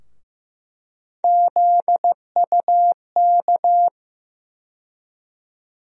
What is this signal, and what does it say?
Morse code, heard as a keyed tone, about 16 wpm, 700 Hz. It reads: ZUK
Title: USA Call Sign Postfixes - 3 Letters 15wpm